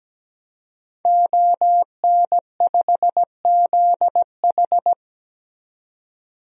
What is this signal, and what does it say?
Morse code, heard as a keyed tone, about 17 wpm, 700 Hz. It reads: ON5ZH
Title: DXCC-100 Call Signs - 5 Characters - 17wpm